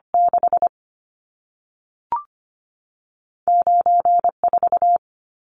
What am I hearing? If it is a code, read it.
Morse code, heard as a keyed tone, about 25 wpm, spaced as for 3 wpm, 700 Hz. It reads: 69 4
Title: Numbers - 2-Digit 25wpm